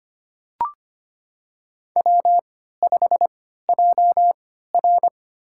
Morse code, heard as a keyed tone, about 25 wpm, spaced as for 14 wpm, 700 Hz. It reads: W5JR
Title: USA Call Signs - 4 Characters - 3x Character Spacing - 25wpm